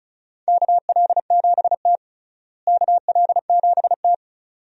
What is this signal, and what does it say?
Morse code, heard as a keyed tone, about 35 wpm, 700 Hz. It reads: KL7T KL7T
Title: USA Call Signs - 4 Characters - Repeated 2x - 35wpm